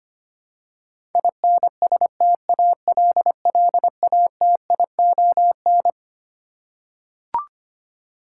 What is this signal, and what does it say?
Morse code, heard as a keyed tone, about 25 wpm, 700 Hz. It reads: INSTALLATION
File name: Twelve Letter Words - 25wpm